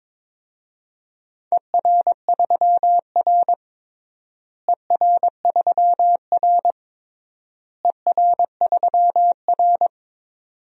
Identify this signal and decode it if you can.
Morse code, heard as a keyed tone, about 21 wpm, 700 Hz. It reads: ER3R ER3R ER3R
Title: DXCC-100 Call Signs - 4 Characters - Repeated 3x - 22wpm